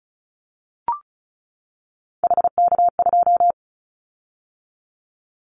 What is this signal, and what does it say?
Morse code, heard as a keyed tone, about 34 wpm, 700 Hz. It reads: HK2
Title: Callsign Prefixes for Top 100 DXCC - 35wpm